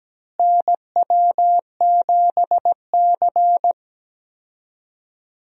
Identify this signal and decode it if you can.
Morse code, heard as a keyed tone, about 17 wpm, 700 Hz. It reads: NW7C
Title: USA Call Signs - 4 Characters - Repeated 3x - 17wpm